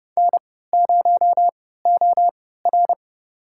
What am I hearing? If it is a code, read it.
Morse code, heard as a keyed tone, about 30 wpm, spaced as for 17 wpm, 700 Hz. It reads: N0OR
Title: USA Call Signs - 4 Characters - 3x Character Spacing - 30wpm